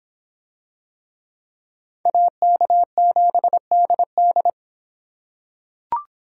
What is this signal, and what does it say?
Morse code, heard as a keyed tone, about 25 wpm, 700 Hz. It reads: AK7DD